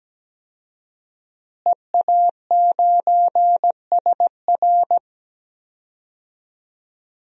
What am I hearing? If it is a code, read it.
Morse code, heard as a keyed tone, about 17 wpm, 700 Hz. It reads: EA9SR
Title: DXCC-100 Call Signs - 5 Characters - Repeated 2x - 17wpm